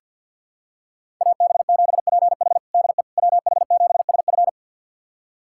Morse code, heard as a keyed tone, about 50 wpm, 700 Hz. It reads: AB7PH DE WH7SF
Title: USA Callsign DE Callsign - 5 Characters - 50wpm